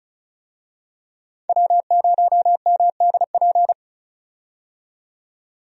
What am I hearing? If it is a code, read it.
Morse code, heard as a keyed tone, about 36 wpm, 700 Hz. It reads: W0MDP
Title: USA Call Signs - 5 Characters - 35wpm